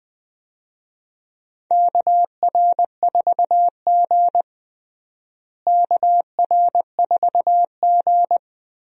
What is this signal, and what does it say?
Morse code, heard as a keyed tone, about 21 wpm, 700 Hz. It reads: KR4G KR4G
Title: USA Call Signs - 4 Characters - Repeated 3x - 20wpm